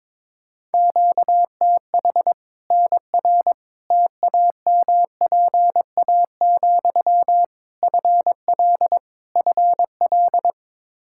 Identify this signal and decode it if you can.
Morse code, heard as a keyed tone, about 21 wpm, 700 Hz. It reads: QTH NR TAMPA, FL FL
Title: K1USN SST QSO Practice 22wpm